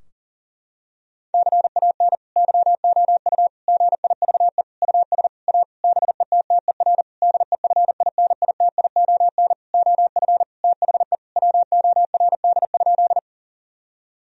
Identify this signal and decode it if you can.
Morse code, heard as a keyed tone, about 40 wpm, 700 Hz. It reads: CAN YOU GIVE US A BETTER DEFINITION OF THE WORD?
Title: Sentences from Top 1000 Words - Review Part 4 - 40wpm